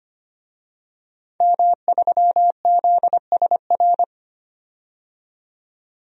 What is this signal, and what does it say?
Morse code, heard as a keyed tone, about 25 wpm, 700 Hz. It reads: M3ZSR
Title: DXCC-100 Call Signs - 5 Characters - 25wpm